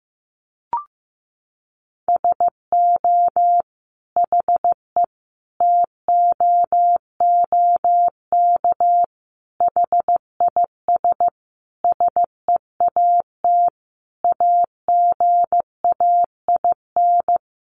Morse code, heard as a keyed tone, about 14 wpm, 700 Hz. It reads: SO HE TOOK HIS SEAT AGAIN
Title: Sentences from Top 1500 Words 15wpm